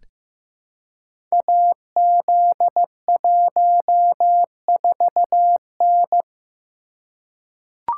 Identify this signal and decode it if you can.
Morse code, heard as a keyed tone, about 15 wpm, 700 Hz. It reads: AZ14N
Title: Code Groups - 5 Letter:Numbers - 15wpm